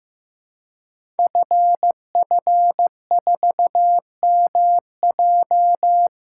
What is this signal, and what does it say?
Morse code, heard as a keyed tone, about 15 wpm, 700 Hz. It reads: FF4MJ
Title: Code Groups - 5 Letter:Numbers - 15wpm